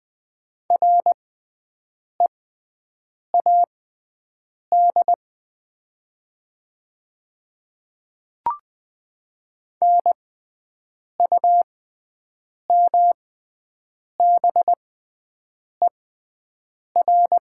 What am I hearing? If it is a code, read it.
Morse code, heard as a keyed tone, about 20 wpm, spaced as for 7 wpm, 700 Hz. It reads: READ NUMBER
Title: 6x Character Spacing - Top 500 Words - 20wpm